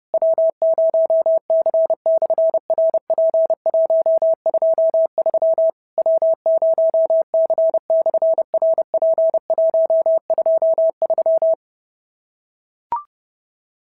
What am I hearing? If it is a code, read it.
Morse code, heard as a keyed tone, about 30 wpm, 650 Hz. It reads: W0C/RP123 W0C/RP123
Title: SOTA QSO Practice 30wpm